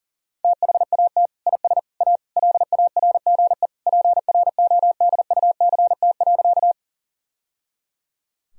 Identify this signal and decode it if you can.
Morse code, heard as a keyed tone, about 40 wpm, 700 Hz. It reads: THAT IS A LARGE PRODUCT.
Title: Sentences from Top 1000 Words - Review Part 2 - 40wpm